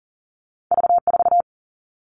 Morse code, heard as a keyed tone, about 40 wpm, 700 Hz. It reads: V 4
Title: Two Challenging Letter-Number Combinations 40wpm